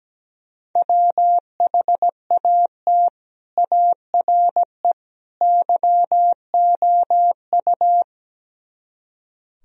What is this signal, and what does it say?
Morse code, heard as a keyed tone, about 17 wpm, 700 Hz. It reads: WHAT ARE YOU
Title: Speed-Racing - Sets of 3 Words 17wpm